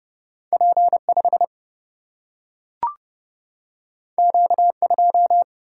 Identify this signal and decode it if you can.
Morse code, heard as a keyed tone, about 30 wpm, 700 Hz. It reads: P 5 Q 2
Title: Single-Letter Single-Number 30wpm